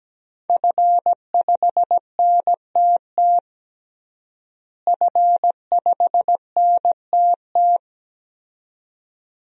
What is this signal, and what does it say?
Morse code, heard as a keyed tone, about 17 wpm, 700 Hz. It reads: F5NTT F5NTT
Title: DXCC-100 Call Signs - 5 Characters - Repeated 2x - 17wpm